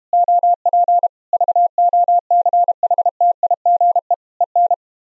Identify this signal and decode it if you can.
Morse code, heard as a keyed tone, about 31 wpm, 700 Hz. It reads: OP VOCHTIGE EN
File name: Zendamateurnieuws en een stukje geseinde morse